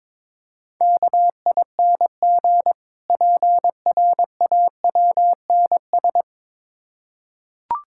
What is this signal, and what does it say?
Morse code, heard as a keyed tone, about 22 wpm, 690 Hz. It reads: KING PRAWNS
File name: Difficult Words with Hints - 22wpm